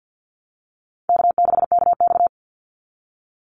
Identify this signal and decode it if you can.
Morse code, heard as a keyed tone, about 50 wpm, 700 Hz. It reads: K6KX
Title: Easy Call Signs Repeated 2x - 50wpm